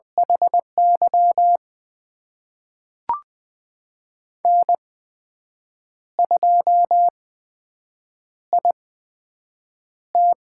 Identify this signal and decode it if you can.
Morse code, heard as a keyed tone, about 20 wpm, spaced as for 7 wpm, 700 Hz. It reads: HY N2IT
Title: USA Call Signs - 8x Character Spacing - 20wpm